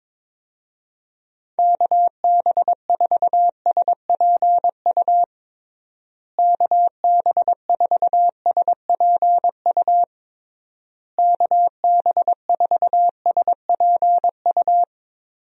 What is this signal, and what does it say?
Morse code, heard as a keyed tone, about 22 wpm, 700 Hz. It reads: KB4SPU KB4SPU KB4SPU
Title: USA Call Signs - 6 Characters - Repeated 3x - 22wpm